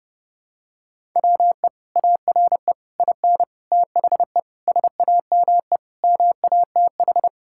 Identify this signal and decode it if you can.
Morse code, heard as a keyed tone, about 32 wpm, 700 Hz. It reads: WE ARE IN THE SAME MATH
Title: Sentences from Top 1000 Words - Review Part 6 - 30wpm